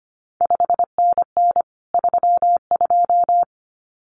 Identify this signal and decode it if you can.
Morse code, heard as a keyed tone, about 25 wpm, 700 Hz. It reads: 5NN 32
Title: CQ WW DX Contest Exchange - 25wpm